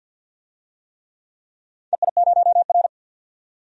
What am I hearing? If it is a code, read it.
Morse code, heard as a keyed tone, about 46 wpm, 700 Hz. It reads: EI0R
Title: DXCC-100 Call Signs - 4 Characters - 50wpm